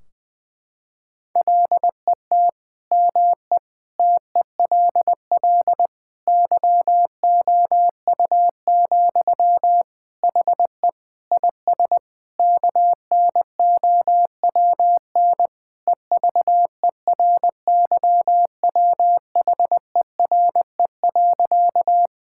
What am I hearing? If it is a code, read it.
Morse code, heard as a keyed tone, about 21 wpm, 700 Hz. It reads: LET ME TELL YOU, HE IS KNOWN EVERYWHERE.
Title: Sentences from Top 1600 Words 20wpm